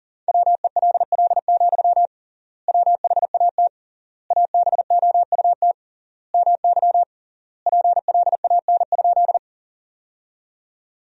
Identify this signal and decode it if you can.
Morse code, heard as a keyed tone, about 40 wpm, 700 Hz. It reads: WELL, WHAT ABOUT MY PLAN?
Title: 3x Word-Spacing - Sentences from Top 500 Words - 40wpm